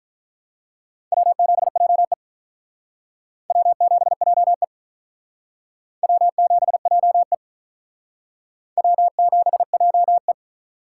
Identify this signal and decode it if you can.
Morse code, heard as a keyed tone, about 41 wpm, 700 Hz. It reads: W7JE W7JE W7JE W7JE
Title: Speed Racing - USA 1x2 Call Signs 35wpm